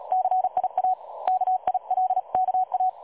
Morse code, heard as a keyed tone, about 38 wpm, 740 Hz. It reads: CIA KILKA
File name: dlazaawansowanych / nagraniazpasma / websdr recording 3565.0kHz